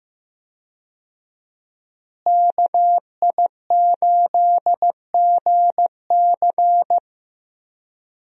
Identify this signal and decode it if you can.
Morse code, heard as a keyed tone, about 15 wpm, 700 Hz. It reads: KI8GC
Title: USA Call Signs - 5 Characters - Repeated 3x - 15wpm